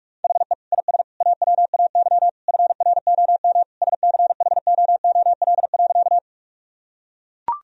Morse code, heard as a keyed tone, about 44 wpm, 700 Hz. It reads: HE IS AWAY FROM SCHOOL.
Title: Sentences from Top 1000 Words - Review Part 1 - 45wpm